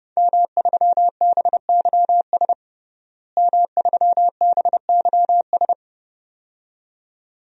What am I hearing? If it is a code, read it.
Morse code, heard as a keyed tone, about 30 wpm, 700 Hz. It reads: M3BYS M3BYS
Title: DXCC-100 Call Signs - 5 Characters - Repeated 2x - 30wpm